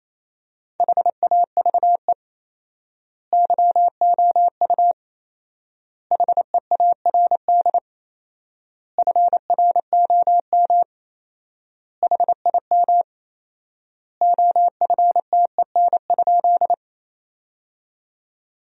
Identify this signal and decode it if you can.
Morse code, heard as a keyed tone, about 27 wpm, 700 Hz. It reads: HAVE YOU HEARD FROM HIM OFTEN?
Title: 4x Word-Spacing - Sentences from Top 400 Words - 28wpm